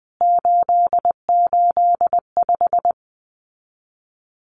Morse code, heard as a keyed tone, about 20 wpm, 700 Hz. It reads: 885
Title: CWT Exchange 20wpm